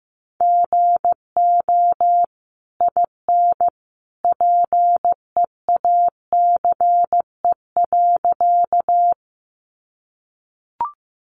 Morse code, heard as a keyed tone, about 15 wpm, 700 Hz. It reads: GO IN PEACE.
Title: Sentences from Top 1500 Words 15wpm